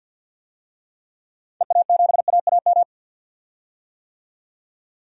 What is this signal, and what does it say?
Morse code, heard as a keyed tone, about 50 wpm, 700 Hz. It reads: EA7AAM
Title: DXCC-100 Call Signs - 6 Characters - 50wpm